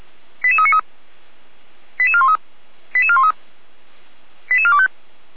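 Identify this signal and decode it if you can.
Five-tone selective call (ZVEI), heard as quick runs of five tones, tones of about 70 ms. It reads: 88393 88413 88413 88425
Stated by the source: Five-tone ZVEI selective call received on an SDR.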